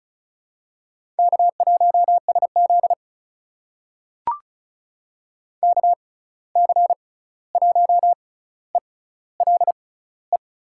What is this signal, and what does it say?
Morse code, heard as a keyed tone, about 36 wpm, spaced as for 16 wpm, 700 Hz. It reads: K1SZ KC1ELE
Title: USA Call Signs - 6x Character Spacing - 35wpm